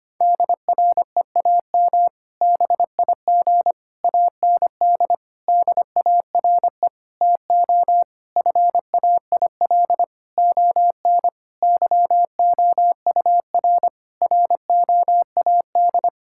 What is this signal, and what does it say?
Morse code, heard as a keyed tone, about 25 wpm, 700 Hz. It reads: DREAM BIG AND DARE TO FAIL ON YOUR ROAD
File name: practice / 25wpm / Difficult Sentences with Hints - 25wpm